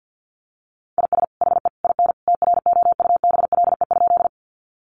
Morse code, heard as a keyed tone, about 50 wpm, 700 Hz. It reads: IS HE IN TROUBLE?
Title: Sentences from Top 1200 Words 50wpm